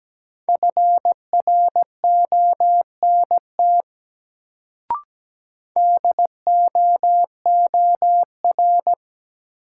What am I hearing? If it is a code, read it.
Morse code, heard as a keyed tone, about 17 wpm, 700 Hz. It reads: FRONT DOOR
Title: Sets of 2 Words - New Word Drill - 17wpm